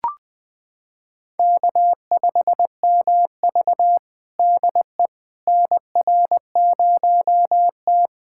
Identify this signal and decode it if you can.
Morse code, heard as a keyed tone, about 20 wpm, 700 Hz. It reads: K5MV DE NR0T
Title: USA Callsign DE Callsign - 4 Characters - 20wpm